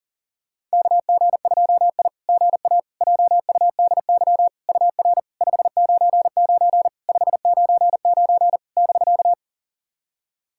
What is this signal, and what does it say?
Morse code, heard as a keyed tone, about 37 wpm, 660 Hz. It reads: KG2I GA JUDY UR 599 599 <BK>
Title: SOTA QSO Practice 40wpm